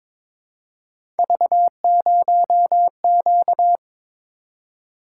Morse code, heard as a keyed tone, about 22 wpm, 700 Hz. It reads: V0Q
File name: practice / 22wpm / Three Letter-Number Combinations - 22wpm